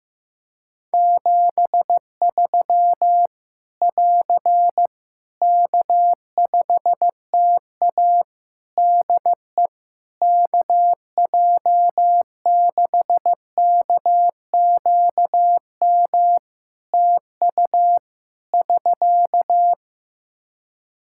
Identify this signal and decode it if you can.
Morse code, heard as a keyed tone, about 15 wpm, 700 Hz. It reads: 73 <AR> K5TA DE KJ6KQM TU <SK>
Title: Practice QSO - Exchange 4 - Ending - 15wpm